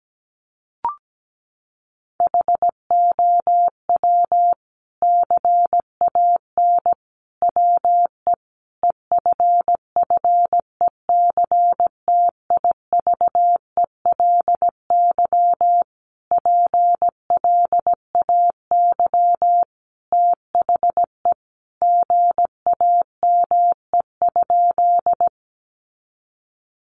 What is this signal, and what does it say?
Morse code, heard as a keyed tone, about 17 wpm, 700 Hz. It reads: HOW CAN WE EFFECTIVELY PLAY THE GAME?
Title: Sentences from Top 1400 Words 17wpm